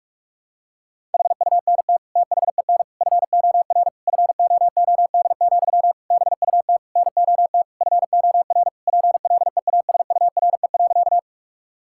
Morse code, heard as a keyed tone, about 41 wpm, 700 Hz. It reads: HUNT THEN FOR FOOD, BUT NOT FOR PLEASURE.
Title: Sentences from Top 1600 Words 45wpm